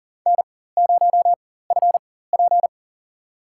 Morse code, heard as a keyed tone, about 38 wpm, spaced as for 18 wpm, 700 Hz. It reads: N0FP
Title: USA Call Signs - 4 Characters - 4x Character Spacing - 40wpm